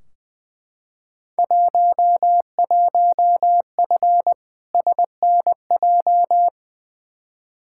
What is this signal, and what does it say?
Morse code, heard as a keyed tone, about 20 wpm, 700 Hz. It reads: 11F SNJ
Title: Field Day Exchanges Encore - 20wpm